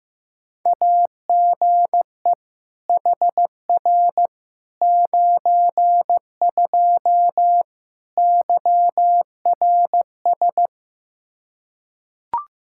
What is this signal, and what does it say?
Morse code, heard as a keyed tone, about 15 wpm, 700 Hz. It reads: AGE HR 92 YRS
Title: Practice QSO - Exchange 3 - More Info - 15wpm